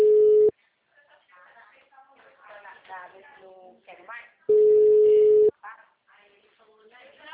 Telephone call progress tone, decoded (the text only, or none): ringback tone